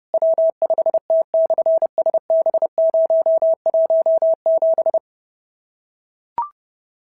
Morse code, heard as a keyed tone, about 30 wpm, 690 Hz. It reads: W5T/SB017
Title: SOTA QSO Practice 30wpm